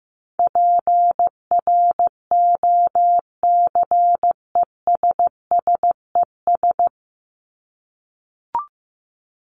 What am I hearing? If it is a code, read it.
Morse code, heard as a keyed tone, about 15 wpm, 700 Hz. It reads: PROCESSES
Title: ES:S Suffix - Top 100 Words with Base Words 15wpm